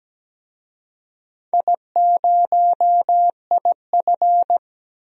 Morse code, heard as a keyed tone, about 17 wpm, 700 Hz. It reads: I0IF
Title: DXCC-100 Call Signs - 4 Characters - 17wpm